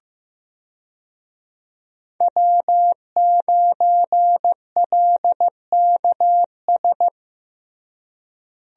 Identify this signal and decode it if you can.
Morse code, heard as a keyed tone, about 15 wpm, 700 Hz. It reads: W9LKS
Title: USA Call Signs - 5 Characters - Repeated 2x - 15wpm